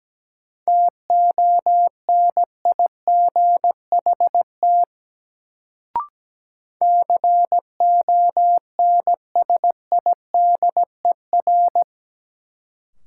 Morse code, heard as a keyed tone, about 17 wpm, 700 Hz. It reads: TONIGHT CONSIDER
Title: Sentences from Top 600 Words - New Word Drill - 17wpm